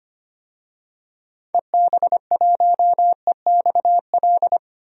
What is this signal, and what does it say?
Morse code, heard as a keyed tone, about 25 wpm, 700 Hz. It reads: EB1EXL
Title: DXCC-100 Call Signs - 6 Characters - 25wpm